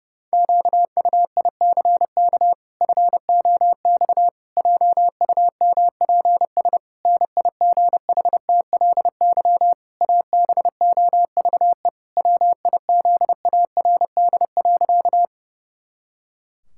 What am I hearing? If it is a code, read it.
Morse code, heard as a keyed tone, about 30 wpm, 700 Hz. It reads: QUICK FOX JUMPS NIGHTLY ABOVE WIZARD.
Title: Pangram Challenge v2 - 30wpm